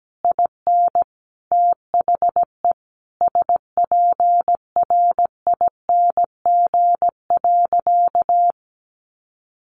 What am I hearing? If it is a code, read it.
Morse code, heard as a keyed tone, about 17 wpm, 700 Hz. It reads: IN THE SPRING.
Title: Sentences from Top 1200 Words 17wpm